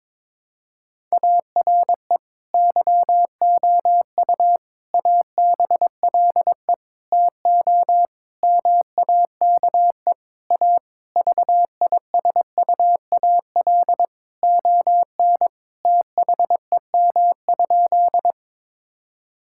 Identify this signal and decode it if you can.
Morse code, heard as a keyed tone, about 22 wpm, 700 Hz. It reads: ARE YOU ABLE TO MAKE A VISUAL ON THEM?
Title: Sentences from Top 1000 Words - Review Part 6 - 22wpm